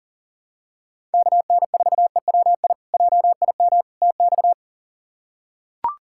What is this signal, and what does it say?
Morse code, heard as a keyed tone, about 40 wpm, 690 Hz. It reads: KN4EWI JIM TX
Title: K1USN SST QSO Practice - 40wpm